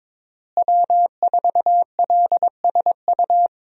Morse code, heard as a keyed tone, about 22 wpm, 700 Hz. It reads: W4LSU
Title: USA Call Signs - 5 Characters - Repeated 3x - 22wpm